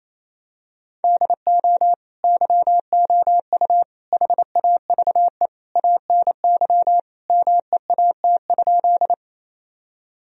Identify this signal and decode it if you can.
Morse code, heard as a keyed tone, about 28 wpm, 700 Hz. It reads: DO YOU HAVE ANY MEAT?
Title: Sentences from Top 1000 Words - Review Part 3 - 28wpm